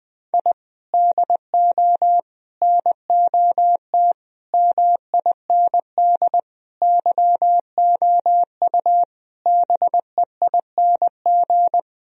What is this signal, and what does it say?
Morse code, heard as a keyed tone, about 20 wpm, 700 Hz. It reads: I DO NOT MIND YOU BEING
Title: Sentences from Top 1000 Words - Review Part 3 - 20wpm